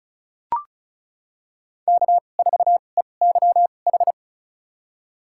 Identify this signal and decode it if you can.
Morse code, heard as a keyed tone, about 36 wpm, spaced as for 25 wpm, 700 Hz. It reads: K4EYH
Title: USA Call Signs - 2x Character Spacing - 35wpm